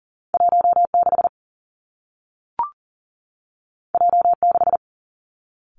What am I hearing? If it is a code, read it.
Morse code, heard as a keyed tone, about 39 wpm, 700 Hz. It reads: 1 6 J 6
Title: Two Challenging Letter-Number Combinations 40wpm